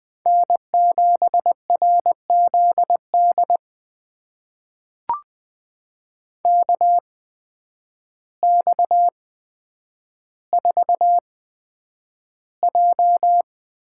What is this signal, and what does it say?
Morse code, heard as a keyed tone, about 20 wpm, spaced as for 9 wpm, 700 Hz. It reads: N7RZD KX4J
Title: USA Call Signs - 5 Characters - 8x Character Spacing - 20wpm